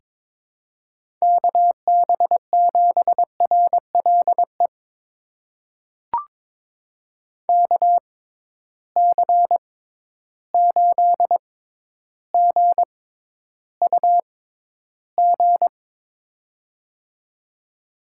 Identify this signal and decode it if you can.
Morse code, heard as a keyed tone, about 22 wpm, spaced as for 11 wpm, 700 Hz. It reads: KB7RLE KC8GUG
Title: USA Call Signs - 6 Characters - 6x Character Spacing - 22wpm